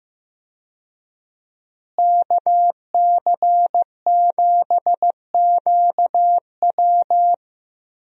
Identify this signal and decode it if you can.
Morse code, heard as a keyed tone, about 15 wpm, 700 Hz. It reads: KC7QW
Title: USA Call Signs - 5 Characters - Repeated 3x - 15wpm